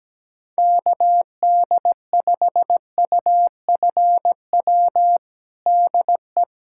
Morse code, heard as a keyed tone, about 17 wpm, 700 Hz. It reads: KD5UFW DE
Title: USA Callsign DE Callsign - 6 Characters - 17wpm